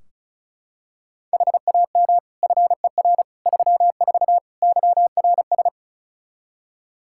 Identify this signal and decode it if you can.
Morse code, heard as a keyed tone, about 36 wpm, 700 Hz. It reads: HAM FER 34 YRS
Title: Practice QSO - Exchange 3 - More Info - 35wpm